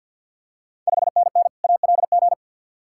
Morse code, heard as a keyed tone, about 53 wpm, 690 Hz. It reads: 5NN ALG
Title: Ontario QSO Party Exchange 50wpm